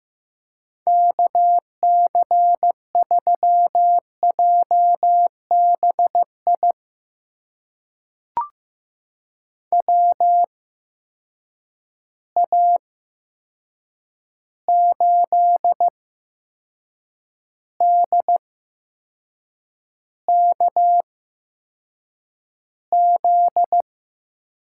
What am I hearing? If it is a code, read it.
Morse code, heard as a keyed tone, about 15 wpm, spaced as for 6 wpm, 700 Hz. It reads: KC3JBI WA8DKZ
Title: USA Call Signs - 6 Characters - 8x Character Spacing - 15wpm